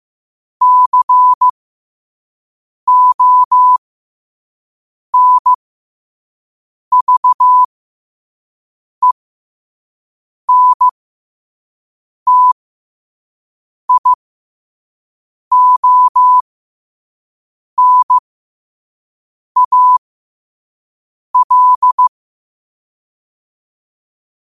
Morse code, heard as a keyed tone, about 15 wpm, spaced as for 5 wpm, 1000 Hz. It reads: CONVENTIONAL
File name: qotdfiles / 5wpm